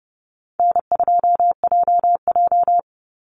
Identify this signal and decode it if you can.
Morse code, heard as a keyed tone, about 30 wpm, 700 Hz. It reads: N2JJ
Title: Call Signs - 30wpm